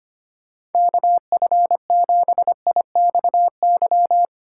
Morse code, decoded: KF7IXY

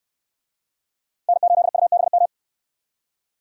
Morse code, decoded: N8UBM

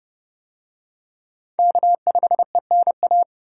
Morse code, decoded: K5ENA

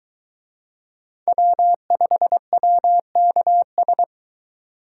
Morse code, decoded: W5WKS

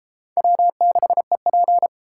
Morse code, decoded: W6EP

